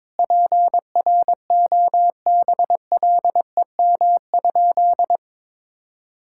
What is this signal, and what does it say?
Morse code, heard as a keyed tone, about 22 wpm, 700 Hz. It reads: PROBLEM?
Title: Sentences from Top 1000 Words - Review Part 6 - 22wpm